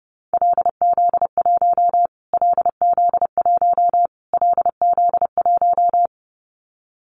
Morse code, heard as a keyed tone, about 30 wpm, 700 Hz. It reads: LZ1 LZ1 LZ1
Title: Callsign Prefixes for Top 25 DXCC - Repeated 3x - 30wpm